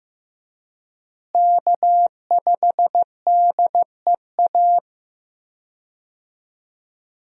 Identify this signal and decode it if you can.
Morse code, heard as a keyed tone, about 15 wpm, 700 Hz. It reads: K5DEA
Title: USA Call Signs - 5 Characters - Repeated 2x - 15wpm